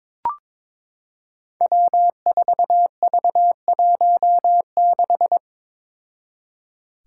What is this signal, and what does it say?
Morse code, heard as a keyed tone, about 22 wpm, 700 Hz. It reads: W4V16
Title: Five Challenging Letter-Number Combinations 22wpm